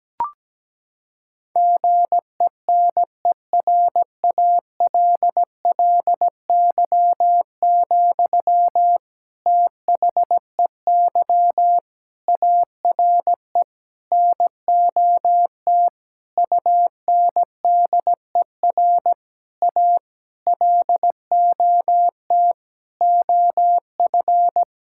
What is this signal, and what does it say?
Morse code, heard as a keyed tone, about 17 wpm, 700 Hz. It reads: GENERALLY, THEY ARE NOT UNDER A LOT OF
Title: Sentences from Top 1000 Words - Review Part 6 - 17wpm